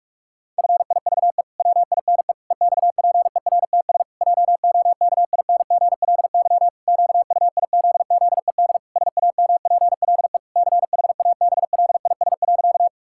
Morse code, heard as a keyed tone, about 41 wpm, 700 Hz. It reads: FIVE WINE EXPERTS JOKINGLY QUIZZED SAMPLE CHABLIS.
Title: Pangram Challenge v2 - 45wpm